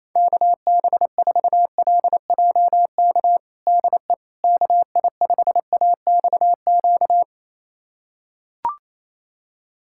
Morse code, heard as a keyed tone, about 27 wpm, 700 Hz. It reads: KB4LJK DE KI5AXQ
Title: USA Callsign DE Callsign - 6 Characters - 28wpm